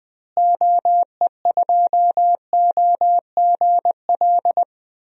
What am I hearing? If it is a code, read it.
Morse code, heard as a keyed tone, about 20 wpm, 700 Hz. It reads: OE2OGL
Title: DXCC-100 Call Signs - 6 Characters - Repeated 3x - 20wpm